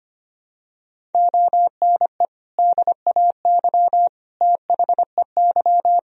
Morse code, heard as a keyed tone, about 25 wpm, 700 Hz. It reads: ONE DAY THEY
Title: Sentences from Top 1000 Words - Review Part 1 - 25wpm